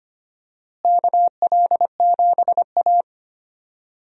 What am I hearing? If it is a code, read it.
Morse code, heard as a keyed tone, about 25 wpm, 700 Hz. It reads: KL7A
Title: DXCC-100 Call Signs - 4 Characters - Repeated 3x - 25wpm